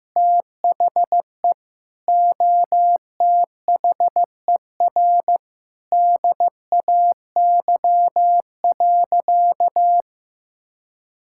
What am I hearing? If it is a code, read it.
Morse code, heard as a keyed tone, about 15 wpm, 700 Hz. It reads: THE OTHER DAY.
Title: Sentences from Top 1000 Words - Review Part 5 - 15wpm